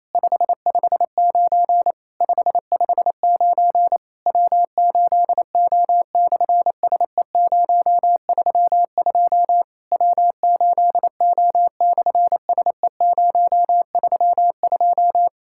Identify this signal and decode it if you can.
Morse code, heard as a keyed tone, about 27 wpm, 700 Hz. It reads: 559 559 W8O/SE032 W8O/SE032
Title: SOTA QSO Practice 28wpm